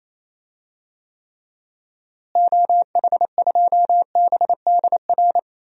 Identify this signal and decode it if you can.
Morse code, heard as a keyed tone, about 27 wpm, 700 Hz. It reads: OH2BDR
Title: DXCC-100 Call Signs - 6 Characters - Repeated 2x - 28wpm